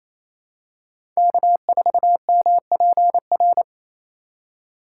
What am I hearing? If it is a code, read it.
Morse code, heard as a keyed tone, about 28 wpm, 700 Hz. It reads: K4MPR
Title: USA Call Signs - 5 Characters - 4x Character Spacing - 28wpm